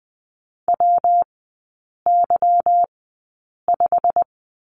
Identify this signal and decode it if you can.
Morse code, heard as a keyed tone, about 20 wpm, spaced as for 8 wpm, 700 Hz. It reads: WY5
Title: 178 Lesson 29e - Slash - Three Characters 20wpm